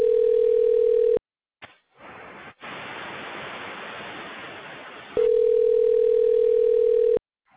Ringback tone (North America), 440 + 480 Hz.